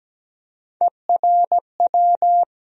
Morse code, heard as a keyed tone, about 17 wpm, 700 Hz. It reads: ERW
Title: USA Call Sign Postfixes - 3 Letters 17wpm